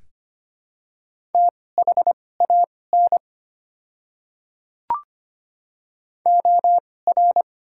Morse code, heard as a keyed tone, about 25 wpm, spaced as for 18 wpm, 700 Hz. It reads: THAN OR
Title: 2x Character Spacing - Top 300 Words - 25wpm